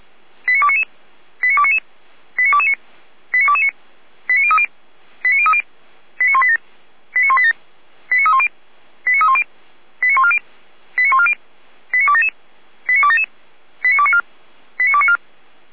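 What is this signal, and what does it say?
Five-tone selective call (ZVEI), heard as quick runs of five tones, tones of about 70 ms. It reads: 89299 89299 89229 89229 89030 89030 89187 89187 89310 89310 89140 89140 89277 89277 89294 89294
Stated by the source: received on SDR, ZVEI five-tone selective call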